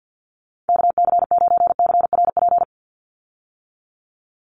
Morse code, heard as a keyed tone, about 49 wpm, 700 Hz. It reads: KC9CRP
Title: USA Call Signs 50wpm